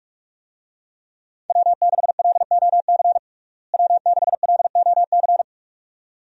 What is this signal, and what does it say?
Morse code, heard as a keyed tone, about 44 wpm, 700 Hz. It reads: W6LOC W6LOC
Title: USA Call Signs - 5 Characters - Repeated 2x - 45wpm